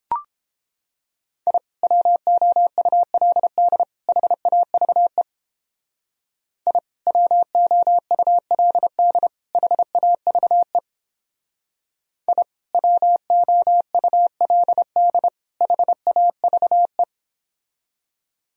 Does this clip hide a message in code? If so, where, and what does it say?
Yes: Morse code from 1.5 to 17.0 s, heard as a keyed tone, about 29 wpm, 700 Hz. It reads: I WOULD HAVE I WOULD HAVE I WOULD HAVE